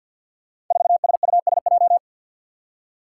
Morse code, heard as a keyed tone, about 48 wpm, 700 Hz. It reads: 4SUSJ
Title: Five Challenging Letter-Number Combinations 50wpm